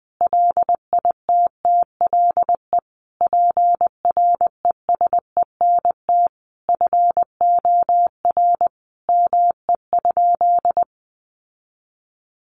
Morse code, heard as a keyed tone, about 20 wpm, 700 Hz. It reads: LITTLE PRESENT FOR ME?
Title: Sentences from Top 800 Words 20wpm